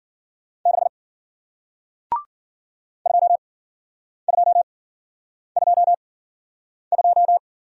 Morse code, heard as a keyed tone, about 53 wpm, spaced as for 9 wpm, 700 Hz. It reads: 6 2222